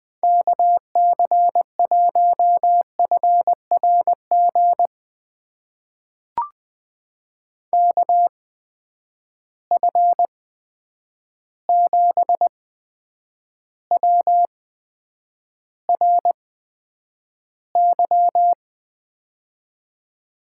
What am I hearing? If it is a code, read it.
Morse code, heard as a keyed tone, about 20 wpm, spaced as for 9 wpm, 700 Hz. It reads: KC1FRG KF7WRY
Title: USA Call Signs - 6 Characters - 8x Character Spacing - 20wpm